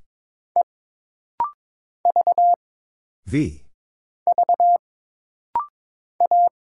Morse code, heard as a keyed tone, about 22 wpm, 700 Hz. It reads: E V V A
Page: This rapid-fire timing is geared to push your ICR (Instant Character Recognition) skills.